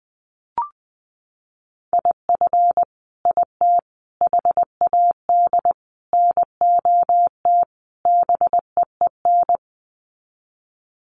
Morse code, heard as a keyed tone, about 20 wpm, 700 Hz. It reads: IF IT HAD NOT BEEN
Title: Sets of 5 Words Encore 20wpm